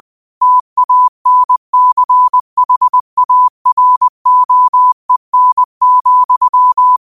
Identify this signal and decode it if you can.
Morse code, heard as a keyed tone, about 20 wpm, 1000 Hz. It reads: TANCHAROEN,
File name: qotdfiles / 20wpm